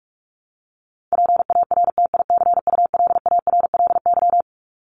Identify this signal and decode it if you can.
Morse code, heard as a keyed tone, about 44 wpm, 700 Hz. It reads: PARTICULARLY
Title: LY Suffix - Top 100 Words with Base Words 45wpm